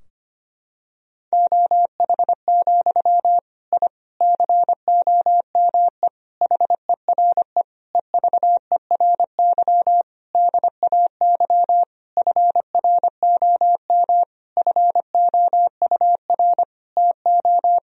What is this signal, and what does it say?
Morse code, heard as a keyed tone, about 25 wpm, 700 Hz. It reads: OH, I COME HERE EVERY DAY FROM FOUR TO
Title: Sentences from Top 1000 Words - Review Part 2 - 25wpm